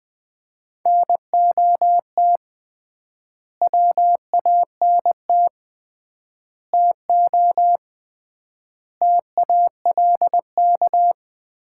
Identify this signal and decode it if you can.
Morse code, heard as a keyed tone, about 20 wpm, 700 Hz. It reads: NOT WANT TO TALK
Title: 3x Word-Spacing - Sets of 5 Words - 20wpm